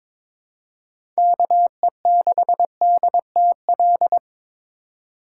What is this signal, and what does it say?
Morse code, heard as a keyed tone, about 22 wpm, 700 Hz. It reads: KE6DTL
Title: USA Call Signs - 2x Character Spacing - 22wpm